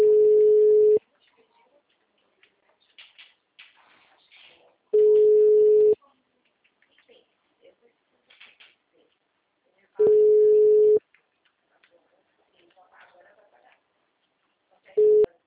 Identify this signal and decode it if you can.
Ringback tone (Europe, 425 Hz), with speech.
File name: out- -2016- - - .24507